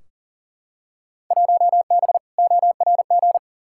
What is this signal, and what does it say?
Morse code, heard as a keyed tone, about 40 wpm, 700 Hz. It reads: 1B ORG
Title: Field Day Exchanges Encore - 40wpm